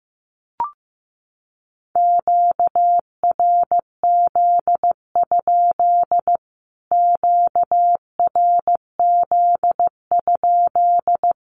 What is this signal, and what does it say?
Morse code, heard as a keyed tone, about 15 wpm, 700 Hz. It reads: QRZ? QRZ?
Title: Q-Codes Repeated 3x 15wpm